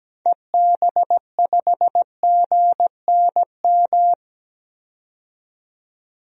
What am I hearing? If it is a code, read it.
Morse code, heard as a keyed tone, about 17 wpm, 700 Hz. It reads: EB5GNM
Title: DXCC-100 Call Signs - 6 Characters - 17wpm